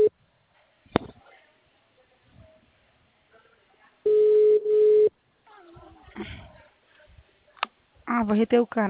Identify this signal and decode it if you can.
Ringback tone (Europe, 425 Hz), with speech.
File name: out- -2006- - - .18721